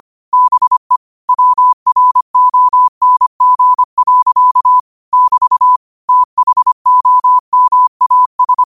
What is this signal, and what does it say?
Morse code, heard as a keyed tone, about 25 wpm, 1000 Hz. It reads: BE WRONG. <BT> THOMAS
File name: qotdfiles / 25wpm